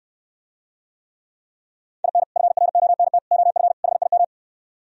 Morse code, heard as a keyed tone, about 67 wpm, 700 Hz. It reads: IN FRONT OF HIM